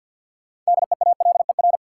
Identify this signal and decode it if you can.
Morse code, heard as a keyed tone, about 50 wpm, 700 Hz. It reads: DEALER
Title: Sentences from Top 1100 Words - New Word Drill Alt - 50wpm